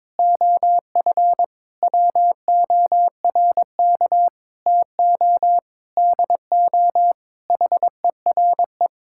Morse code, heard as a keyed tone, about 22 wpm, 700 Hz. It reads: OF WORK TO DO HERE
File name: Sentences with 7-Word Phrases - 22wpm